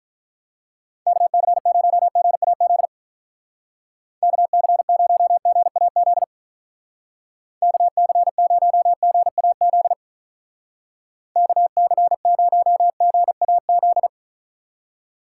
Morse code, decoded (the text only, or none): KC0GAZ KC0GAZ KC0GAZ KC0GAZ